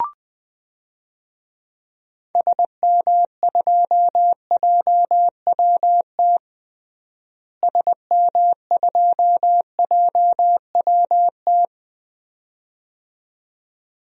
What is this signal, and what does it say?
Morse code, heard as a keyed tone, about 20 wpm, 700 Hz. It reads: SM2JWT SM2JWT
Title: DXCC-100 Call Signs - 6 Characters - Repeated 2x - 20wpm